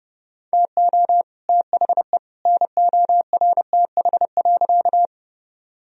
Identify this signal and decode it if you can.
Morse code, heard as a keyed tone, about 30 wpm, 700 Hz. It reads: TO THE NORTH.
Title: Sentences from Top 1000 Words - Review Part 1 - 30wpm